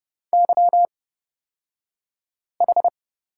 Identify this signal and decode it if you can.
Morse code, heard as a keyed tone, about 30 wpm, spaced as for 5 wpm, 700 Hz. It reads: YH
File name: Lesson 36d - BT - Two Characters 30wpm